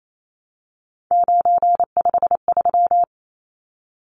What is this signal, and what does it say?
Morse code, heard as a keyed tone, about 28 wpm, 700 Hz. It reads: 953
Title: Numbers - 3-Digit 28wpm